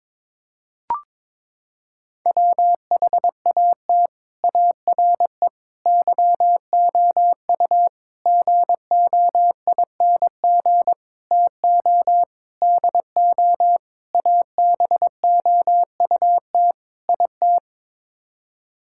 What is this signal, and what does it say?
Morse code, heard as a keyed tone, about 21 wpm, 700 Hz. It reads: WHAT ARE YOU GOING TO DO ABOUT IT
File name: practice / 22wpm / Sets of 8 Words - 22wpm